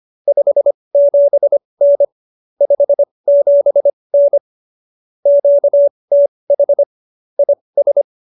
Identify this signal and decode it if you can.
Morse code, heard as a keyed tone, about 25 wpm, spaced as for 20 wpm, 570 Hz. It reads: 57N 57N QTH IS
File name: QSO313 20